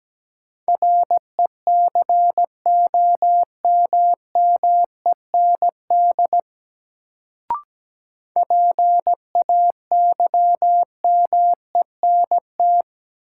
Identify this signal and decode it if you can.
Morse code, heard as a keyed tone, about 17 wpm, 700 Hz. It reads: RECOMMEND PAYMENT
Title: Sentences from Top 1200 Words - New Word Drill - 17wpm